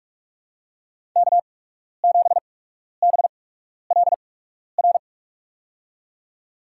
Morse code, heard as a keyed tone, about 46 wpm, spaced as for 12 wpm, 700 Hz. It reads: K7BLR
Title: USA Call Signs - 5 Characters - 8x Character Spacing - 45wpm